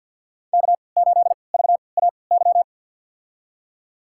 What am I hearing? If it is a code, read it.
Morse code, heard as a keyed tone, about 52 wpm, spaced as for 29 wpm, 690 Hz. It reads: K8VAY
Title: USA Call Signs - 3x Character Spacing - 50wpm